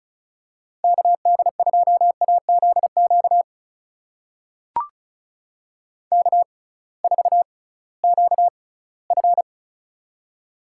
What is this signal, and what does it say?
Morse code, heard as a keyed tone, about 36 wpm, spaced as for 20 wpm, 700 Hz. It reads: KD2AZQ K4QF
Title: USA Call Signs - 6x Character Spacing - 35wpm